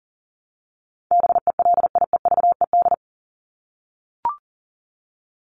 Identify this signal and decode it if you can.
Morse code, heard as a keyed tone, about 39 wpm, 700 Hz. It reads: BELIEVED
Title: ED Suffix - Top 100 Words with Base Words 40wpm